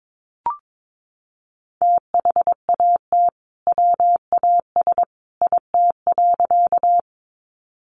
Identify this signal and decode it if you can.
Morse code, heard as a keyed tone, about 22 wpm, 700 Hz. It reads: THAT WAS IT.
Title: Sentences from Top 100 Words 22wpm